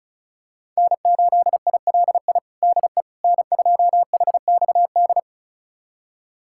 Morse code, heard as a keyed tone, about 36 wpm, 700 Hz. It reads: N8ILI DE N2HXD
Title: USA Callsign DE Callsign - 5 Characters - 35wpm